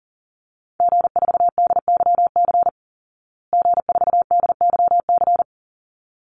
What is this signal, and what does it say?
Morse code, heard as a keyed tone, about 40 wpm, 700 Hz. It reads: G4DYC G4DYC
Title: Call Signs Repeated 3x - 40wpm